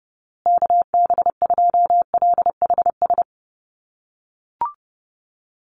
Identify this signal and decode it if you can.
Morse code, heard as a keyed tone, about 30 wpm, 700 Hz. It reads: KB2LHS
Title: USA Call Signs Repeated 3x 30wpm